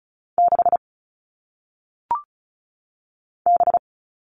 Morse code, heard as a keyed tone, about 36 wpm, 700 Hz. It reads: 6 B
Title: All Cut Numbers - 1-Digit 35wpm